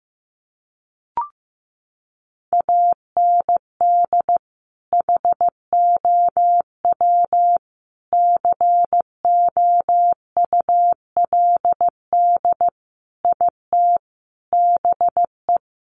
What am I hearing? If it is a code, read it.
Morse code, heard as a keyed tone, about 15 wpm, 700 Hz. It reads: AND HOW COULD IT BE
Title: Sentences from Top 1200 Words 15wpm